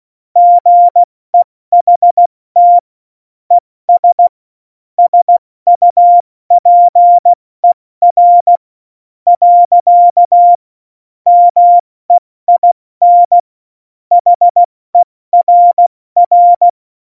Morse code, heard as a keyed tone, about 16 wpm, spaced as for 14 wpm, 700 Hz. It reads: GEHT ES SUPER . MEIN HERR
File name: Morsekurs / Witze / Witz 06. Oktober 2020, 70 F80